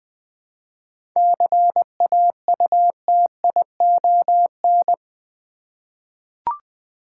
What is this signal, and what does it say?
Morse code, heard as a keyed tone, about 20 wpm, 700 Hz. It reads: CAUTION
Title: 7 Letter Words - First 3 Letters Given - 20wpm